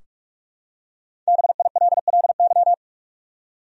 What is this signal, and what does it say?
Morse code, heard as a keyed tone, about 46 wpm, 700 Hz. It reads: BILLY
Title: USA Common Names - Max 5 Letters - 45wpm